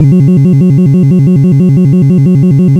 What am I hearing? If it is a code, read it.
There is no code.